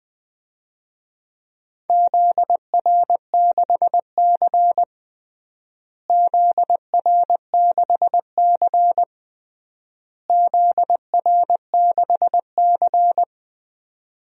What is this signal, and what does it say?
Morse code, heard as a keyed tone, about 21 wpm, 700 Hz. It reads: ZR6C ZR6C ZR6C
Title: DXCC-100 Call Signs - 4 Characters - Repeated 3x - 20wpm